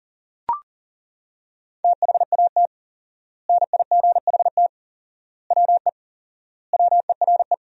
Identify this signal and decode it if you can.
Morse code, heard as a keyed tone, about 40 wpm, 700 Hz. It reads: THAT NIGHT WE WERE
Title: 4x Word-Spacing - Sentences from Top 300 Words - 40wpm